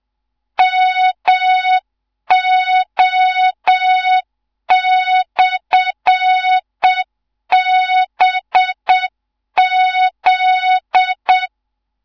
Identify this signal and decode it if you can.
Morse code, heard as a keyed tone, about 7 wpm, 3700 Hz. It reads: MO/BZ
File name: Morze-HA2MN / hang / 15 GYAKORLAT